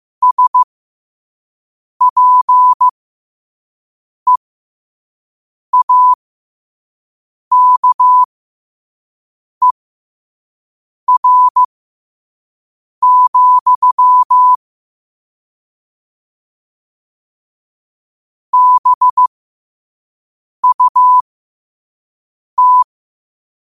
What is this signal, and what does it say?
Morse code, heard as a keyed tone, about 15 wpm, spaced as for 5 wpm, 1000 Hz. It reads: SPEAKER, BUT
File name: qotdfiles / 5wpm